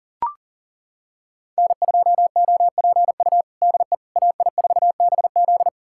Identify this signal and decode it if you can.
Morse code, heard as a keyed tone, about 37 wpm, 700 Hz. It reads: N2OPU DE AI4BZ
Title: USA Callsign DE Callsign 40wpm